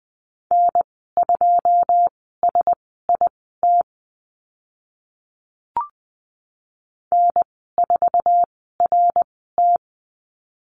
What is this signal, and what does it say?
Morse code, heard as a keyed tone, about 20 wpm, spaced as for 14 wpm, 700 Hz. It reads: N2SIT N4RT
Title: 082 Lesson 14g - Number 2 - Call Signs 20wpm